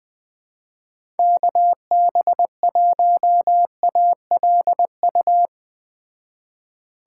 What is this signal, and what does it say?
Morse code, heard as a keyed tone, about 20 wpm, 700 Hz. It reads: KB1ALU
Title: Speed Racing - USA 2x3 Call Signs 17wpm